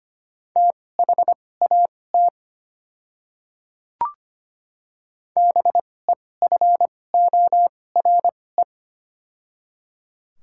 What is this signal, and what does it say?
Morse code, heard as a keyed tone, about 25 wpm, spaced as for 18 wpm, 700 Hz. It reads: THAT BEFORE
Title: 2x Character Spacing - Top 200 Words - 25wpm